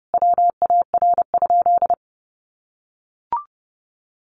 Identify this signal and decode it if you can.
Morse code, heard as a keyed tone, about 28 wpm, 700 Hz. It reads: WAR?
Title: Sentences from Top 1400 Words 30wpm